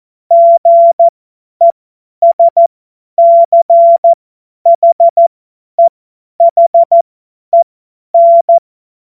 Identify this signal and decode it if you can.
Morse code, heard as a keyed tone, about 14 wpm, spaced as for 10 wpm, 680 Hz. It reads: GESCHEHEN
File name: Morsekurs / Witze / Witz 26. April 2022 50 F70